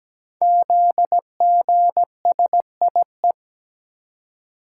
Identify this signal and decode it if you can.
Morse code, heard as a keyed tone, about 17 wpm, 700 Hz. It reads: ZGSIE
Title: Code Groups - 5 Letters - 17wpm